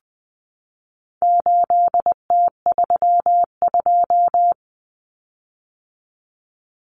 Morse code, heard as a keyed tone, about 20 wpm, 700 Hz. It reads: 8T32
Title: Common Cut Numbers - 4-Digit 20wpm